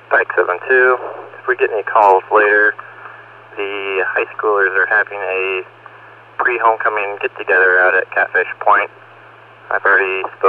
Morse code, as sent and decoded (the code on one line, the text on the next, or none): none